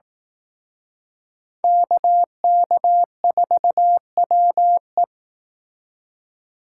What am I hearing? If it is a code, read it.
Morse code, heard as a keyed tone, about 18 wpm, 700 Hz. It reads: KK4WE
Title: Speed Racing - USA 2x2 Call Signs 15wpm